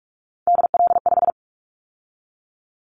Morse code, heard as a keyed tone, about 46 wpm, 700 Hz. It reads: DL5
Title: Callsign Prefixes for Top 50 DXCC - 45wpm